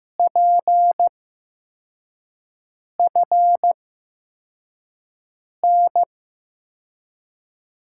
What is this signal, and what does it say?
Morse code, heard as a keyed tone, about 16 wpm, spaced as for 4 wpm, 700 Hz. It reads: PFN
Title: USA Call Signs - 6 Characters - 8x Character Spacing - 15wpm